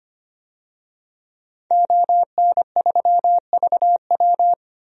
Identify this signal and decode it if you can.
Morse code, heard as a keyed tone, about 25 wpm, 700 Hz. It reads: ON3VW
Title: DXCC-100 Call Signs - 5 Characters - 25wpm